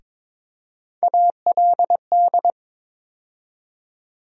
Morse code, heard as a keyed tone, about 22 wpm, 700 Hz. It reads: ALD
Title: USA Call Sign Postfixes - 3 Letters 22wpm